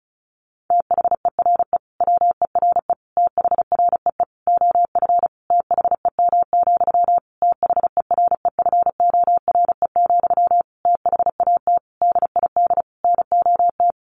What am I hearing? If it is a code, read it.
Morse code, heard as a keyed tone, about 34 wpm, 700 Hz. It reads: THERE WERE THREE OF THEM, THEREFORE, THAT DID NOT
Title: Sentences from Top 800 Words 35wpm